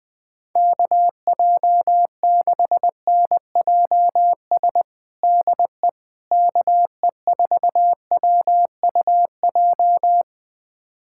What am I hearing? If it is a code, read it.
Morse code, heard as a keyed tone, about 20 wpm, 700 Hz. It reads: KJ6NJS DE KE4WUJ
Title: USA Callsign DE Callsign 20wpm